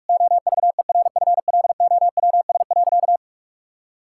Morse code, heard as a keyed tone, about 41 wpm, 700 Hz. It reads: OVERFLOWS.
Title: Pangram Challenge v2 - 45wpm